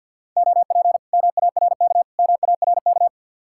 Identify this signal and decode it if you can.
Morse code, heard as a keyed tone, about 60 wpm, 700 Hz. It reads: OP MARK MARK
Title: Speed-Racing Common QSO Phrases - 50wpm